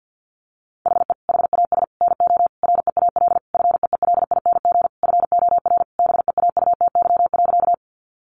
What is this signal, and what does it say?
Morse code, heard as a keyed tone, about 50 wpm, 700 Hz. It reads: HE HAS NO REAL FEELING FOR BEAUTY.
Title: Sentences from Top 1200 Words 50wpm